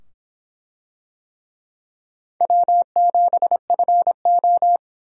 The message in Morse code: W7FO

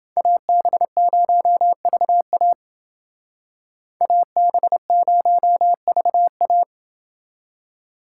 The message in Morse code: AB0VA AB0VA

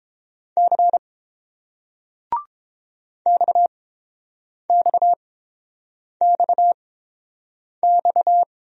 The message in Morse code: C XXXX

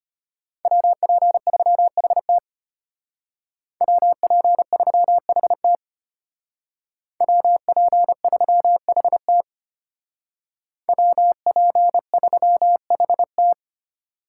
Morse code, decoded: WP3HT WP3HT WP3HT WP3HT

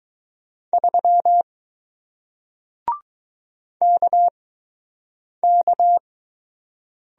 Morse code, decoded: 3 KK

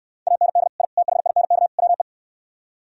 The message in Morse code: AND I THINK WE

